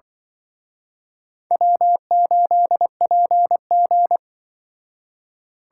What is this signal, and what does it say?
Morse code, heard as a keyed tone, about 24 wpm, 700 Hz. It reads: W8PG
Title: Speed Racing - USA 1x2 Call Signs 20wpm